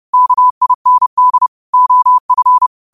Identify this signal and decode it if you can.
Morse code, heard as a keyed tone, about 30 wpm, 1000 Hz. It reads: KIND OF
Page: New quotes every day in morse code at 30 Words per minute.